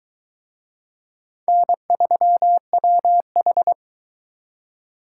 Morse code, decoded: N3WH

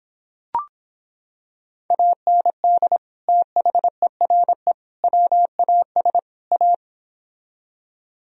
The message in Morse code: AND THERE WAS A

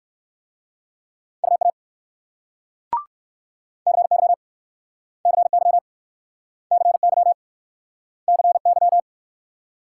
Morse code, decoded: UA CY CY CY CY

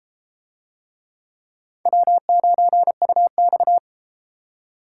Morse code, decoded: W9UX

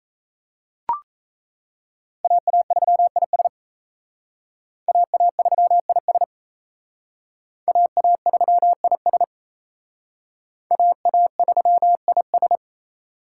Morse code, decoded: AA3IS AA3IS AA3IS AA3IS

